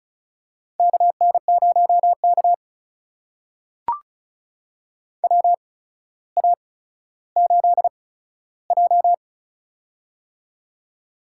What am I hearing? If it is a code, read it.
Morse code, heard as a keyed tone, about 36 wpm, spaced as for 15 wpm, 700 Hz. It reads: KN0K WA8J
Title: USA Call Signs - 4 Characters - 8x Character Spacing - 35wpm